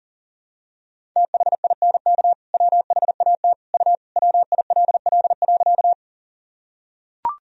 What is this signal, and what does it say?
Morse code, heard as a keyed tone, about 40 wpm, 700 Hz. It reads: THINK WHAT U WILL.
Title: Sentences from Top 200 Words and Abbreviations 40wpm